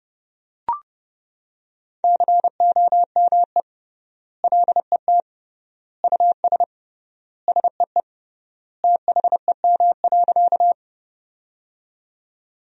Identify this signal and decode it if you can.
Morse code, heard as a keyed tone, about 32 wpm, 700 Hz. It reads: COME LET US SEE THEM.
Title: 3x Word-Spacing - Sentences from Top 300 Words - 30wpm